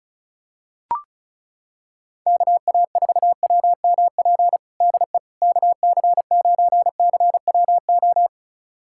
Morse code, heard as a keyed tone, about 34 wpm, 690 Hz. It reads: KA4WMP DE KC9CWO
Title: USA Callsign DE Callsign - 6 Characters - 35wpm